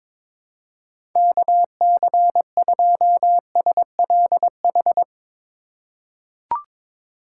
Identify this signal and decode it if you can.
Morse code, heard as a keyed tone, about 21 wpm, 700 Hz. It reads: KC2SLH
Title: USA Call Signs - 6 Characters - 2x Character Spacing - 22wpm